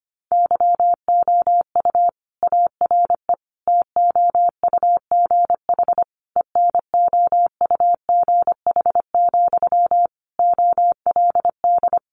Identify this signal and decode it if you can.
Morse code, heard as a keyed tone, about 25 wpm, 700 Hz. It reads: YOU ARE TOUGH ENOUGH, OLD
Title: Sentences from Top 1200 Words 25wpm